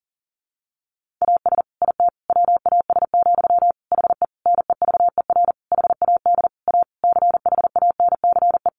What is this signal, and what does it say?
Morse code, heard as a keyed tone, about 40 wpm, 700 Hz. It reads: AS IT WAS, HE NEVER HAD A CHANCE
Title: Sentences from Top 1400 Words 40wpm